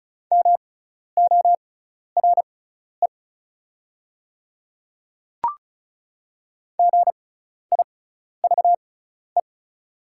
Morse code, heard as a keyed tone, about 36 wpm, spaced as for 12 wpm, 700 Hz. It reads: MORE GIVE
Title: 6x Character Spacing - Top 300 Words - 35wpm